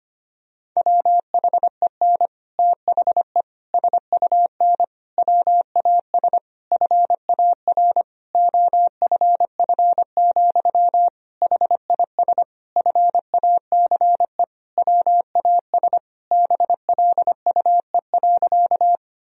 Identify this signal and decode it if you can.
Morse code, heard as a keyed tone, about 25 wpm, 700 Hz. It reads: WHEN THE SUN WAS FAR OFF, HIS FACE WAS BLUE.
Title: Sentences from Top 1000 Words - Review Part 3 - 25wpm